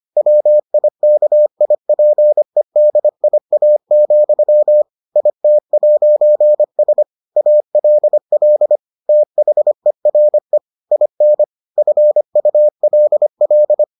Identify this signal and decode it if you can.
Morse code, heard as a keyed tone, about 25 wpm, 600 Hz. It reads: WIKIPEDIA, IT'S ALL THERE IN FULL